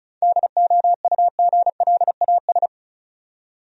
Morse code, heard as a keyed tone, about 34 wpm, 700 Hz. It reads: DOUGLAS
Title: USA Common Names - 35wpm